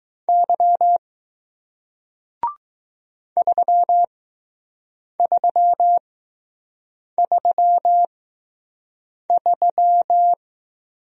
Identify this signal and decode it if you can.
Morse code, heard as a keyed tone, about 20 wpm, spaced as for 6 wpm, 700 Hz. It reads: Y 3333